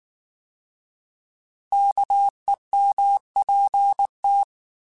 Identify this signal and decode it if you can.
Morse code, heard as a keyed tone, about 19 wpm, 800 Hz. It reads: KEMPT